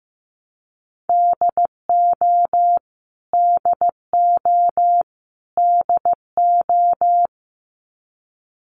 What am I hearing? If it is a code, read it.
Morse code, heard as a keyed tone, about 15 wpm, 700 Hz. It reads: DO DO DO
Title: Top 200 Words Repeated 15wpm